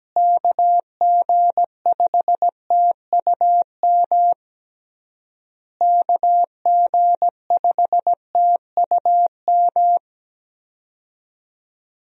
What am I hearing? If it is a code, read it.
Morse code, heard as a keyed tone, about 17 wpm, 700 Hz. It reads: KG5TUM KG5TUM
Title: USA Call Signs - 6 Characters - Repeated 3x - 17wpm